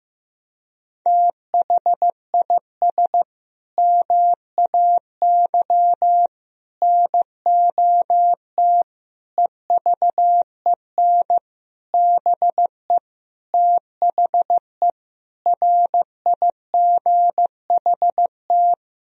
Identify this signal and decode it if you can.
Morse code, heard as a keyed tone, about 15 wpm, 700 Hz. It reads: THIS MAY NOT EVEN BE THE RIGHT
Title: Sentences from Top 1000 Words - Review Part 3 - 15wpm